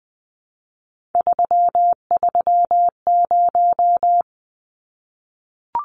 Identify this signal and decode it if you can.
Morse code, heard as a keyed tone, about 19 wpm, 700 Hz. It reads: 330
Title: Numbers - 3-Digit 20wpm